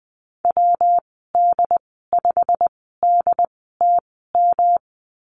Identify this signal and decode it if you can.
Morse code, heard as a keyed tone, about 20 wpm, spaced as for 15 wpm, 700 Hz. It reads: WD5DTM
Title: 122 Lesson 20g - Number 3 - Call Signs 20wpm